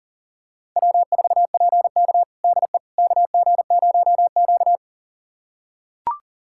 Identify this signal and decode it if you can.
Morse code, heard as a keyed tone, about 40 wpm, 700 Hz. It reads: W4PK DE KG0Q
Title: USA Callsign DE Callsign - 4 Characters - 40wpm